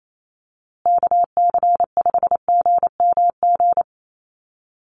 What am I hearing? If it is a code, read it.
Morse code, heard as a keyed tone, about 28 wpm, 700 Hz. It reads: KC5GMG
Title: USA Call Signs Repeated 3x 28wpm